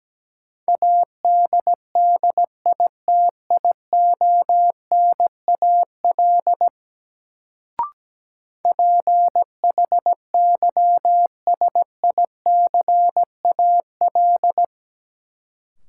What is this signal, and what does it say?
Morse code, heard as a keyed tone, about 17 wpm, 700 Hz. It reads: ADDITIONAL PHYSICAL
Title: Sentences from Top 1000 Words - Review Part 4 - New Word Drill - 17wpm